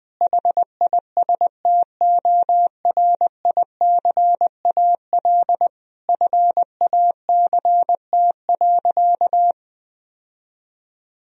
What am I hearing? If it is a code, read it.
Morse code, heard as a keyed tone, about 20 wpm, 700 Hz. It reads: HISTORICAL FACT.
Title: Sentences from Top 1000 Words - Review Part 4 - 20wpm